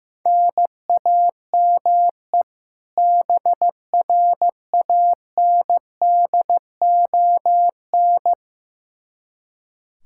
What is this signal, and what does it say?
Morse code, heard as a keyed tone, about 15 wpm, 700 Hz. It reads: NAME BRANDON
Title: K1USN SST QSO Practice 15wpm